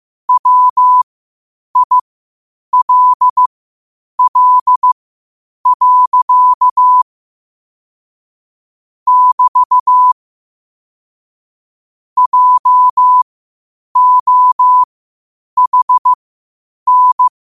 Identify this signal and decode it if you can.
Morse code, heard as a keyed tone, about 15 wpm, spaced as for 8 wpm, 1000 Hz. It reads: WILL. <BT> JOHN
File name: qotdfiles / 8wpm